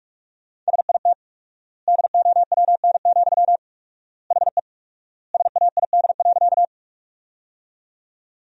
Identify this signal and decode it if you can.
Morse code, heard as a keyed tone, about 46 wpm, 700 Hz. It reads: SIT DOWN, HE SAID.
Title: 4x Word-Spacing - Sentences from Top 500 Words - 45wpm